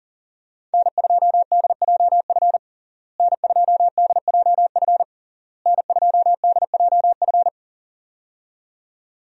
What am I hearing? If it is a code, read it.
Morse code, heard as a keyed tone, about 37 wpm, 700 Hz. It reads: N2DJF N2DJF N2DJF
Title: USA Call Signs - 5 Characters - Repeated 3x - 40wpm